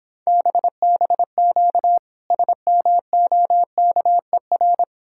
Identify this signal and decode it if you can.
Morse code, heard as a keyed tone, about 26 wpm, 700 Hz. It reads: BBQ SMOKER